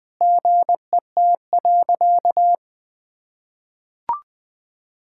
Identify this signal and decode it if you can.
Morse code, heard as a keyed tone, about 20 wpm, 700 Hz. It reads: GET.
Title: Sentences with 8-Word Phrases - Spoken Beginning - 20wpm